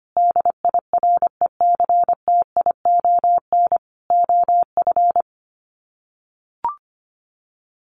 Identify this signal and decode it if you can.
Morse code, heard as a keyed tone, about 25 wpm, 700 Hz. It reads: DIRECTION OF
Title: Sets of 4 Words 25wpm